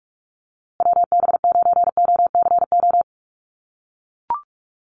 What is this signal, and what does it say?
Morse code, heard as a keyed tone, about 44 wpm, 700 Hz. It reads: WB9OCO
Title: USA Call Signs Repeated 3x 45wpm